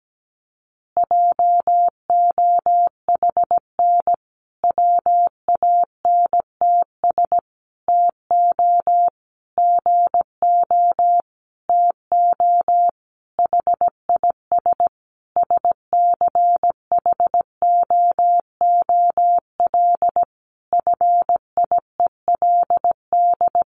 Morse code, heard as a keyed tone, about 17 wpm, 700 Hz. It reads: JOHN WANTS TO GO TO HIS SCHOOL FIELD
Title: Additional Easy Phrases 17wpm